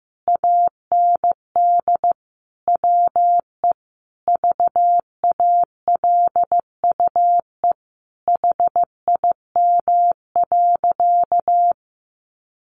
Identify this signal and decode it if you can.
Morse code, heard as a keyed tone, about 15 wpm, 700 Hz. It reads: AND WE VALUE HIM.
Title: Sentences from Top 600 Words 15wpm